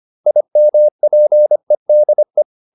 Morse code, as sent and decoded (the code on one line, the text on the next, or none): .. -- .--. -.. .
IMPEDE